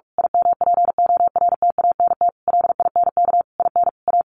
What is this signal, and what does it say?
Morse code, heard as a keyed tone, about 44 wpm, 700 Hz. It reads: IMPORTANT LINK IN A
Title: A Sound Of Thunder 45wpm